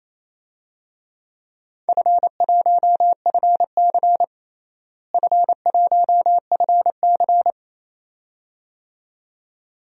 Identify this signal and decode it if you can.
Morse code, heard as a keyed tone, about 28 wpm, 700 Hz. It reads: F1FC F1FC
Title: DXCC-100 Call Signs - 4 Characters - Repeated 2x - 28wpm